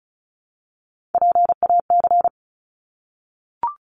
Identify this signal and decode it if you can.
Morse code, heard as a keyed tone, about 34 wpm, 700 Hz. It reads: PAC
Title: ARRL Sections 35wpm